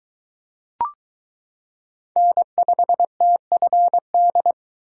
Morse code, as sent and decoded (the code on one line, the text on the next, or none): -. ..... - ..-. -..
N5TFD